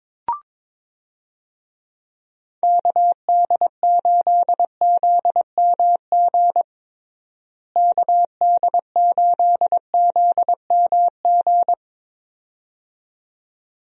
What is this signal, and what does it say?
Morse code, heard as a keyed tone, about 22 wpm, 700 Hz. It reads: KD8ZMG KD8ZMG
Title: USA Call Signs - 6 Characters - Repeated 2x - 22wpm